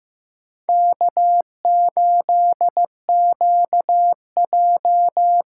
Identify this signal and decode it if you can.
Morse code, heard as a keyed tone, about 15 wpm, 700 Hz. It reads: K8QJ
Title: USA Call Signs - 8x Character Spacing - 15wpm